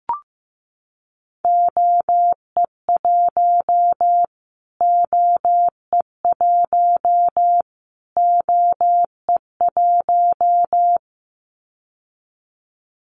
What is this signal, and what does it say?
Morse code, heard as a keyed tone, about 15 wpm, 700 Hz. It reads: OE1 OE1 OE1
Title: Callsign Prefixes for Top 25 DXCC - Repeated 3x - 15wpm